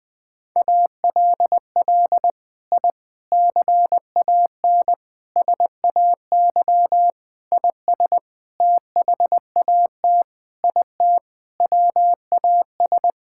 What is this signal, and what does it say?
Morse code, heard as a keyed tone, about 20 wpm, 700 Hz. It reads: ALL I CAN SAY IS THAT IT WAS
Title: Sentences with 6-Word Phrases - Spoken Beginning - 20wpm